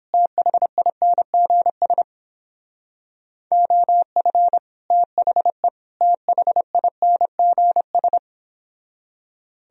Morse code, heard as a keyed tone, about 29 wpm, 700 Hz. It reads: THINGS OF THE THINGS